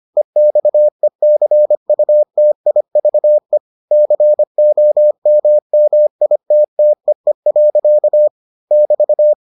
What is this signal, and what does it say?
Morse code, heard as a keyed tone, about 25 wpm, 600 Hz. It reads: EXECUTIVE COMMITTEE. <BT>